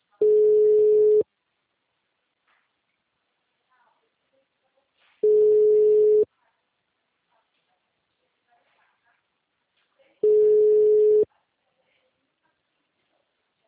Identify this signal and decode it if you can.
Ringback tone (Europe, 425 Hz).